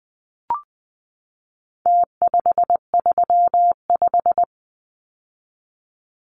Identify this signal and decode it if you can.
Morse code, heard as a keyed tone, about 20 wpm, 700 Hz. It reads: T535
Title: Numbers - Contest Style - 4-Digit 20wpm